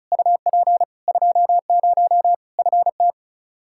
Morse code, heard as a keyed tone, about 36 wpm, 700 Hz. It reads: UP 20 FT
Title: Practice QSO - Exchange 2 - Info - 35wpm